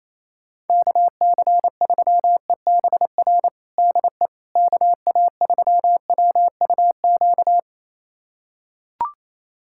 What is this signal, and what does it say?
Morse code, heard as a keyed tone, about 28 wpm, 700 Hz. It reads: KC3EBR DE KA3WUQ
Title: USA Callsign DE Callsign - 6 Characters - 28wpm